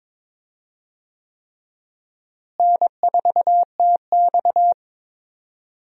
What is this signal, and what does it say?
Morse code, heard as a keyed tone, about 22 wpm, 700 Hz. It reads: N4TX
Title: USA Call Signs - 4 Characters - Repeated 3x - 22wpm